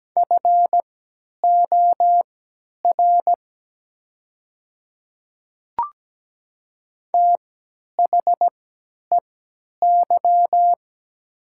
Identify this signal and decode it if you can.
Morse code, heard as a keyed tone, about 17 wpm, spaced as for 10 wpm, 700 Hz. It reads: FOR THEY
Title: 3x Character Spacing - Top 100 Words - 17wpm